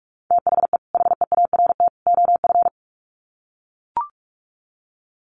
Morse code, heard as a keyed tone, about 44 wpm, 700 Hz. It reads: THE HEART OF
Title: Sets of 3 Words 45wpm